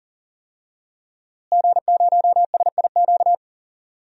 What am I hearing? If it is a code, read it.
Morse code, heard as a keyed tone, about 40 wpm, 700 Hz. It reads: G0SIQ
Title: DXCC-100 Call Signs - 5 Characters - Repeated 2x - 40wpm